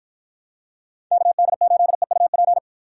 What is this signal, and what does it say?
Morse code, heard as a keyed tone, about 53 wpm, 700 Hz. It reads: KD8EUP